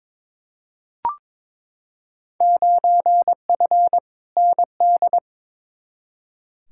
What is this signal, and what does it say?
Morse code, heard as a keyed tone, about 21 wpm, 700 Hz. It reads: 9F ND
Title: Field Day Exchanges Encore - 22wpm